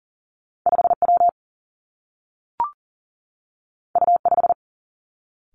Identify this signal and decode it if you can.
Morse code, heard as a keyed tone, about 39 wpm, 700 Hz. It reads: 5W U5
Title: Two Challenging Letter-Number Combinations 40wpm